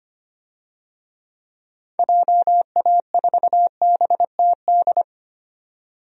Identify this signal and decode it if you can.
Morse code, heard as a keyed tone, about 25 wpm, 700 Hz. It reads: JA4BTD
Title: DXCC-100 Call Signs - 6 Characters - 25wpm